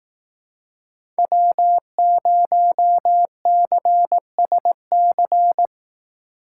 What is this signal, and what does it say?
Morse code, heard as a keyed tone, about 18 wpm, 700 Hz. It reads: W0CSC
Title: Speed Racing - USA Call Signs 15wpm